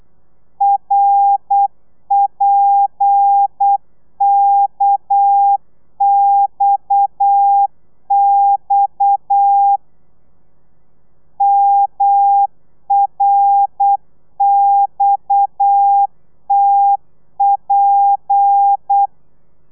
Morse code, heard as a keyed tone, about 8 wpm, 800 Hz. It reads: RPKXX MRXTP